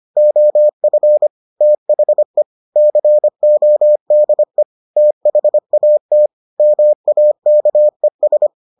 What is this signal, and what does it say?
Morse code, heard as a keyed tone, about 25 wpm, 600 Hz. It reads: OF THE CODE THAT MAKES